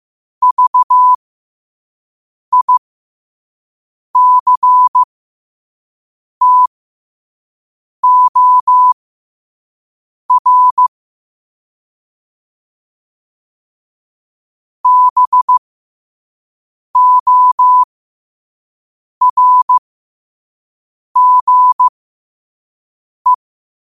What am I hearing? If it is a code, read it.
Morse code, heard as a keyed tone, about 15 wpm, spaced as for 5 wpm, 1000 Hz. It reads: VICTOR BORGE
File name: qotdfiles / 5wpm